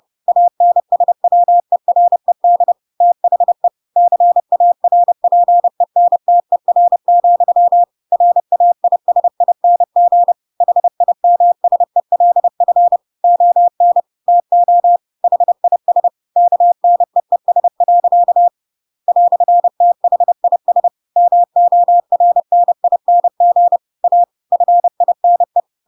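Morse code, heard as a keyed tone, about 30 wpm, 700 Hz. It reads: ANSWERED THE CARPENTER, RAISING HIMSELF ON TO HIS KNEES. "THIS MORNING A FINE
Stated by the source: Chapter 2 of The Adventures of Pinocchio read as a book in Morse code (CW) at 30 wpm, by ik1hge.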